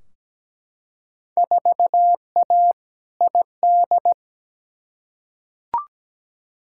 Morse code, heard as a keyed tone, about 17 wpm, 700 Hz. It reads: 4A ID
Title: Field Day Exchanges Encore - 17wpm